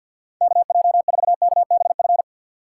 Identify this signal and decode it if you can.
Morse code, heard as a keyed tone, about 52 wpm, 700 Hz. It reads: KJ4KBF
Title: USA Call Signs - 3x Character Spacing - 50wpm